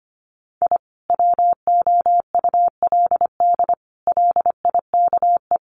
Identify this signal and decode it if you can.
Morse code, heard as a keyed tone, about 25 wpm, 700 Hz. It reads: I WOULD LIKE
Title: Sets of 5 Words Encore2 25wpm